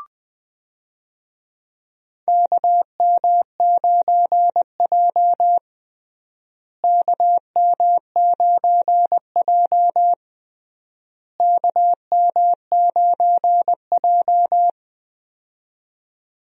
Morse code, heard as a keyed tone, about 20 wpm, 700 Hz. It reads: KM9J KM9J KM9J
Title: USA Call Signs - 4 Characters - Repeated 3x - 20wpm